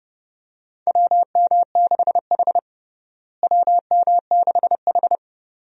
Morse code, heard as a keyed tone, about 30 wpm, 700 Hz. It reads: WM6H WM6H
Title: USA Call Signs - 4 Characters - Repeated 2x - 30wpm